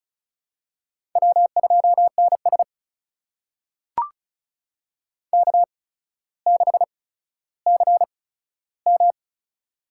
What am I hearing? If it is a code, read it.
Morse code, heard as a keyed tone, about 36 wpm, spaced as for 15 wpm, 700 Hz. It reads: W2NS K6CM
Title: USA Call Signs - 4 Characters - 8x Character Spacing - 35wpm